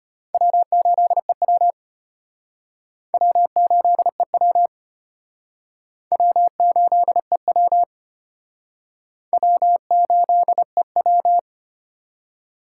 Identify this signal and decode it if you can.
Morse code, heard as a keyed tone, about 31 wpm, 700 Hz. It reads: W8EW W8EW W8EW W8EW
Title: Speed Racing - USA 1x2 Call Signs 25wpm